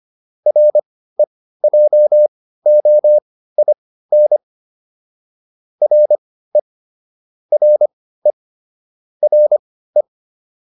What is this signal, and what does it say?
Morse code, heard as a keyed tone, about 25 wpm, spaced as for 15 wpm, 600 Hz. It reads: REJOIN RE RE RE